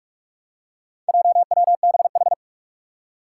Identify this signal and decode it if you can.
Morse code, heard as a keyed tone, about 44 wpm, 700 Hz. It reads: JWBH
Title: Four Challenging Letter-Number Combinations 45wpm